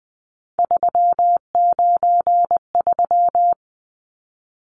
Morse code, heard as a keyed tone, about 20 wpm, 700 Hz. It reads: 393
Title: Numbers - 3-Digit 20wpm